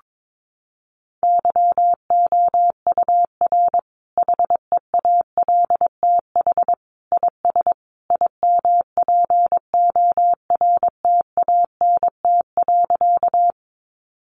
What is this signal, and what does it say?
Morse code, heard as a keyed tone, about 22 wpm, 700 Hz. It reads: YOUR HEALTH IS IMPORTANT.
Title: Sentences from Top 600 Words 22wpm